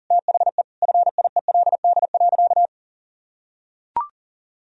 Morse code, heard as a keyed tone, about 37 wpm, 700 Hz. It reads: THE FIELD.
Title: Sentences from Top 1000 Words - Review Part 4 - 40wpm